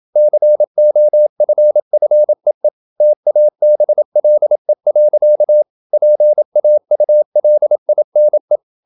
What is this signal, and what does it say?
Morse code, heard as a keyed tone, about 27 wpm, 600 Hz. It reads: COFFEE TABLE. PAULINE
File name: Never 0027wpm